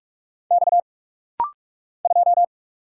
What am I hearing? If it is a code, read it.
Morse code, heard as a keyed tone, about 44 wpm, 690 Hz. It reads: X 2
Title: Single Letter-Number - Rapid-Fire 45wpm